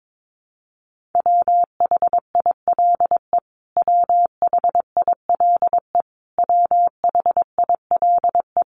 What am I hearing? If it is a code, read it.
Morse code, heard as a keyed tone, about 22 wpm, 700 Hz. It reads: WHILE WHILE WHILE
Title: Top 500 Words Repeated 22wpm